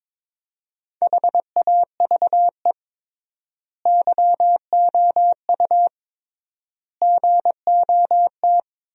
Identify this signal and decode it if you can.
Morse code, heard as a keyed tone, about 22 wpm, 700 Hz. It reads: HAVE YOU GOT
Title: 3x Word-Spacing - Sentences from Top 300 Words - 22wpm